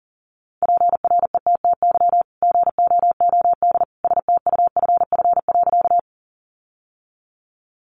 Morse code, heard as a keyed tone, about 40 wpm, 700 Hz. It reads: PRETTY GOOD STUFF.
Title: Sentences from Top 1400 Words 40wpm